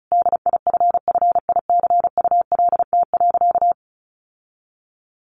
Morse code, heard as a keyed tone, about 34 wpm, 700 Hz. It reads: DIFFICULT.
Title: Sentences from Top 1100 Words 35wpm